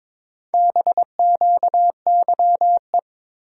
Morse code, heard as a keyed tone, about 22 wpm, 700 Hz. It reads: BQYE
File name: practice / 22wpm / Four Letter-Number Combinations - 22wpm